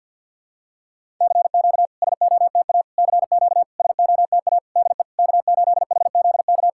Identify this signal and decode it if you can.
Morse code, heard as a keyed tone, about 47 wpm, 690 Hz. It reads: CQ SOTA CQ SOTA DE K8HZK
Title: SOTA QSO Practice 50wpm